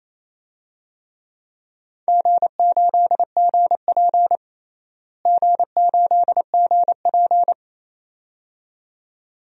Morse code, heard as a keyed tone, about 27 wpm, 700 Hz. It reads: G8GP G8GP
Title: DXCC-100 Call Signs - 4 Characters - Repeated 2x - 28wpm